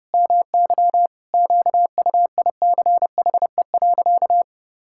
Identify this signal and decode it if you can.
Morse code, heard as a keyed tone, about 32 wpm, 700 Hz. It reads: MY QUICHE.
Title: Pangram Challenge v2 - 30wpm